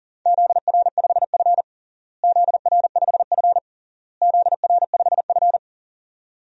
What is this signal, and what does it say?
Morse code, heard as a keyed tone, about 37 wpm, 700 Hz. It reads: ZR5F ZR5F ZR5F
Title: DXCC-100 Call Signs - 4 Characters - Repeated 3x - 40wpm